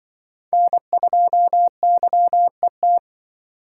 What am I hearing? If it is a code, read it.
Morse code, heard as a keyed tone, about 24 wpm, 700 Hz. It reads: N2YET
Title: Speed Racing - USA Call Signs 20wpm